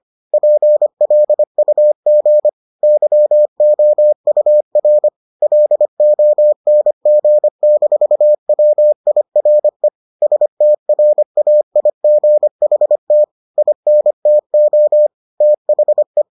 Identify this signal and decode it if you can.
Morse code, heard as a keyed tone, about 25 wpm, 600 Hz. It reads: PLUG YOUR LONG-WIRE STRAIGHT INTO THE